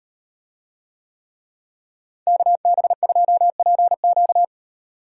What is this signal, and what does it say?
Morse code, heard as a keyed tone, about 38 wpm, 700 Hz. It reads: KB2PQ